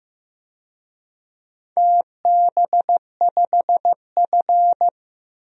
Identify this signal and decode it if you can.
Morse code, heard as a keyed tone, about 16 wpm, 700 Hz. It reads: TB5F
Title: DXCC-100 Call Signs - 4 Characters - Repeated 3x - 15wpm